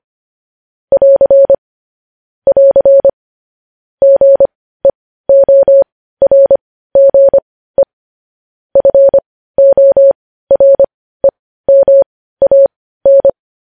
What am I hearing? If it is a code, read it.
Morse code, heard as a keyed tone, about 25 wpm, spaced as for 15 wpm, 550 Hz. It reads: <AR> <AR> GEORGE FOREMAN
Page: news headlines in morse code at 15 wpm